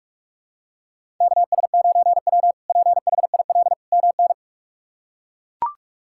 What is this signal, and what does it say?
Morse code, heard as a keyed tone, about 41 wpm, 700 Hz. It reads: KS9W PHIL MN
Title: K1USN SST QSO Practice - 45wpm